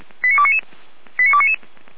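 Five-tone selective call (ZVEI), heard as quick runs of five tones, tones of about 70 ms. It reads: 89299 89299
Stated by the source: Five-tone ZVEI selective call received on an SDR, from the audio.f85360.feuerwehr-orlamuende.org fire brigade server.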